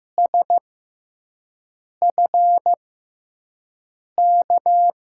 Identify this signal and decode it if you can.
Morse code, heard as a keyed tone, about 15 wpm, spaced as for 5 wpm, 700 Hz. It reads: SFK
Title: USA Call Signs - 6x Character Spacing - 15wpm